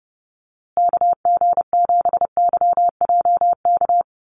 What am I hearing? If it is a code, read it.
Morse code, heard as a keyed tone, about 30 wpm, 700 Hz. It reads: KG7YJK
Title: USA Call Signs Repeated 3x 30wpm